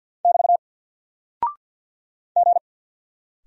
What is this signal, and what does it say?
Morse code, heard as a keyed tone, about 48 wpm, 700 Hz. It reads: <BT> G
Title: Single Character (Letter:Number:Punctuation:Prosign) 50wpm